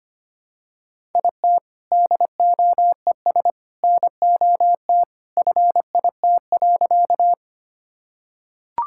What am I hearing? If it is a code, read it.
Morse code, heard as a keyed tone, about 25 wpm, 700 Hz. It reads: IT DOES NOT FIT.
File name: practice / 25wpm / Sentences with 2-Word Phrases - 25wpm